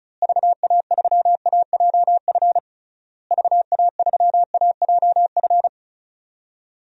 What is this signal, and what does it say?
Morse code, heard as a keyed tone, about 34 wpm, 700 Hz. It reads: VA3AJF VA3AJF
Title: DXCC-100 Call Signs - 6 Characters - Repeated 3x - 35wpm